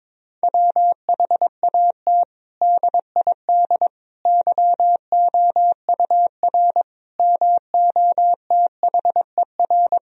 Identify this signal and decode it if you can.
Morse code, heard as a keyed tone, about 22 wpm, 700 Hz. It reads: WHAT DID YOUR MOTHER
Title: Sentences from Top 1000 Words - Review Part 4 - 22wpm